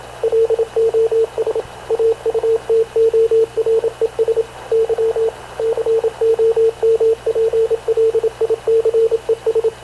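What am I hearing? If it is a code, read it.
Morse code, heard as a keyed tone, about 27 wpm, 450 Hz, with radio noise. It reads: LOS AUTORES Y COMPLICES
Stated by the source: Havana Radio / CLA sending Spanish press (PX) in Morse, recorded by Radio Officer Paul Dumesnil onboard CNAV QUEST/CZDO.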